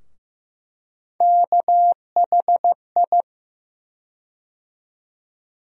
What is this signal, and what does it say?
Morse code, heard as a keyed tone, about 15 wpm, 700 Hz. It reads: KHI
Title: USA Call Sign Postfixes - 3 Letters 15wpm